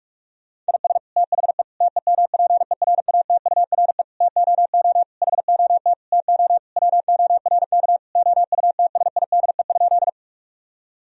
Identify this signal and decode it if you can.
Morse code, heard as a keyed tone, about 46 wpm, 700 Hz. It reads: IS THE TEMPERATURE TOO HOT TO WORK OUTSIDE?
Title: Sentences from Top 1000 Words - Review Part 4 - 45wpm